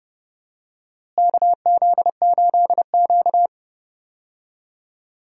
Morse code, heard as a keyed tone, about 32 wpm, 700 Hz. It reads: KZ8Q
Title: USA Call Signs - 4 Characters - Repeated 2x - 30wpm